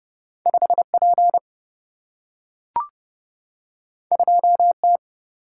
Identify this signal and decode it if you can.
Morse code, heard as a keyed tone, about 30 wpm, 700 Hz. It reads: 5 P 2 T
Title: Single-Letter Single-Number 30wpm